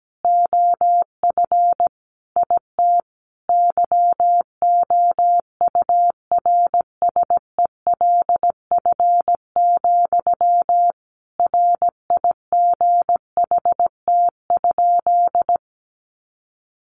Morse code, heard as a keyed tone, about 17 wpm, 700 Hz. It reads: OF IT YOURSELF, RIGHT?
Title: Sentences from Top 1200 Words 17wpm